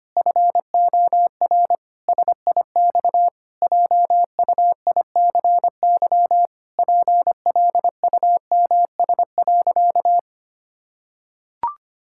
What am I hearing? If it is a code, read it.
Morse code, heard as a keyed tone, about 25 wpm, 700 Hz. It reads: FOR SIX JUICY PLUMS.
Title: Pangram Challenge v2 - 25wpm